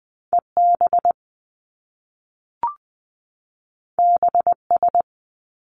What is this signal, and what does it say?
Morse code, heard as a keyed tone, about 20 wpm, 700 Hz. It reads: E B B S
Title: Top Two Letter Combinations in Words 20wpm